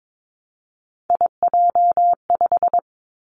Morse code, heard as a keyed tone, about 22 wpm, 700 Hz. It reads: IJ5
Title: Three Challenging Letter-Number Combinations 22wpm